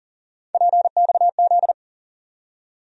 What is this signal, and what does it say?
Morse code, heard as a keyed tone, about 40 wpm, 700 Hz. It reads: PXZ
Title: USA Call Sign Postfixes - 3 Letters 40wpm